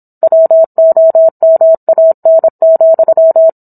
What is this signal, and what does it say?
Morse code, heard as a keyed tone, about 27 wpm, 650 Hz. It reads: WOMAN,